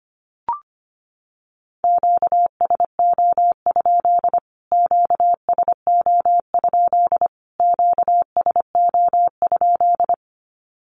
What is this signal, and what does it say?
Morse code, heard as a keyed tone, about 25 wpm, 700 Hz. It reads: QSO? QSO? QSO?
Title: Q-Codes Repeated 3x 25wpm